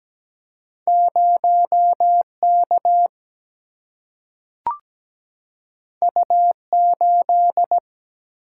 Morse code, heard as a keyed tone, about 17 wpm, 700 Hz. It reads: 0 K U 8
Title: Single-Letter Single-Number 17wpm